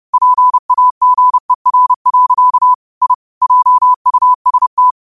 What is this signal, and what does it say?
Morse code, heard as a keyed tone, about 30 wpm, 1000 Hz. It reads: PAGER. I JUST
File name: qotdfiles / 30wpm